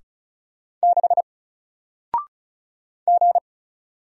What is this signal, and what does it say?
Morse code, heard as a keyed tone, about 36 wpm, 700 Hz. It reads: 6 G
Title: Single Character (Letter:Number:Punctuation:Prosign) Alt 35wpm